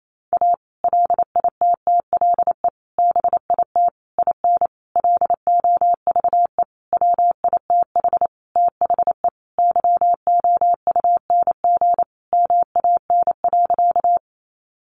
Morse code, heard as a keyed tone, about 28 wpm, 700 Hz. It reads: A LITTLE BIT IN LOVE WITH THE YOUNG MAN.
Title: Sentences from Top 900 Words 28wpm